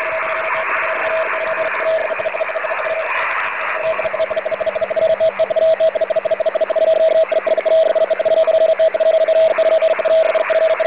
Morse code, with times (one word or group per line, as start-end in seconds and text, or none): none